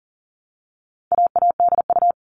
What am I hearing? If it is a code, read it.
Morse code, heard as a keyed tone, about 40 wpm, 700 Hz. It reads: AADU
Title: All Cut Numbers - 4-Digit 40wpm